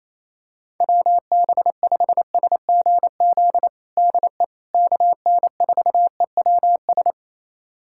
Morse code, heard as a keyed tone, about 28 wpm, 700 Hz. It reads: WB5SGZ DE KN4EWS
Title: USA Callsign DE Callsign 28wpm